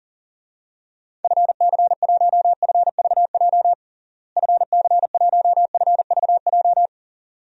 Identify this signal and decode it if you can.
Morse code, heard as a keyed tone, about 37 wpm, 700 Hz. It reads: FC1FVJ FC1FVJ
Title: DXCC-100 Call Signs - 6 Characters - Repeated 2x - 40wpm